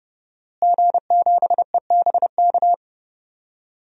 Morse code, heard as a keyed tone, about 30 wpm, 700 Hz. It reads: G7EBK
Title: DXCC-100 Call Signs - 5 Characters - Repeated 2x - 30wpm